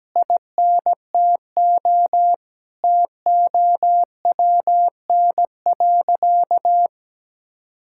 Morse code, heard as a keyed tone, about 17 wpm, 700 Hz. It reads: INTO TOWN.
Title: Sentences from Top 1000 Words - Review Part 6 - 17wpm